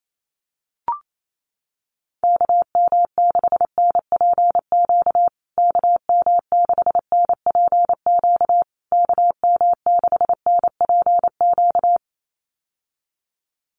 Morse code, heard as a keyed tone, about 28 wpm, 700 Hz. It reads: KM6NPQ KM6NPQ KM6NPQ
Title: USA Call Signs Repeated 3x 28wpm